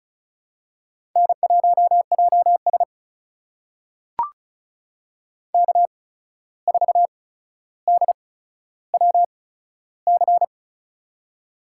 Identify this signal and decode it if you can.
Morse code, heard as a keyed tone, about 36 wpm, spaced as for 14 wpm, 700 Hz. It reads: N1JS K4DWC
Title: USA Call Signs - 8x Character Spacing - 35wpm